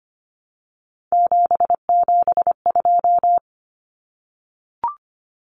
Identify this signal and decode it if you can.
Morse code, heard as a keyed tone, about 25 wpm, 690 Hz. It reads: 772
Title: Numbers - 3-Digit 25wpm